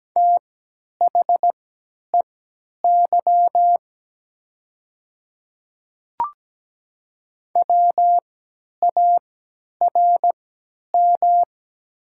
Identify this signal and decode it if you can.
Morse code, heard as a keyed tone, about 17 wpm, spaced as for 10 wpm, 700 Hz. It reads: THEY WARM
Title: 3x Character Spacing - Top 500 Words - 17wpm